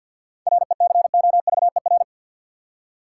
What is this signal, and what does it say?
Morse code, heard as a keyed tone, about 53 wpm, 700 Hz. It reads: RECOVER
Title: Seven Letter Words - 50wpm